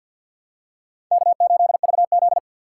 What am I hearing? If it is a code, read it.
Morse code, heard as a keyed tone, about 52 wpm, 700 Hz. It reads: K8VZ
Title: USA Call Signs - 4 Characters - 2x Character Spacing - 50wpm